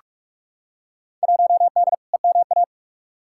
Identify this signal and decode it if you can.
Morse code, heard as a keyed tone, about 43 wpm, 700 Hz. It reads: 1D EMA
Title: Field Day Exchanges - 45wpm